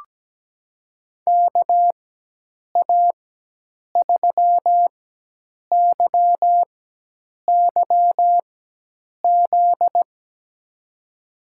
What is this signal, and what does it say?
Morse code, heard as a keyed tone, about 17 wpm, spaced as for 8 wpm, 700 Hz. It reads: KA3YYZ
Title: USA Call Signs - 6 Characters - 4x Character Spacing - 17wpm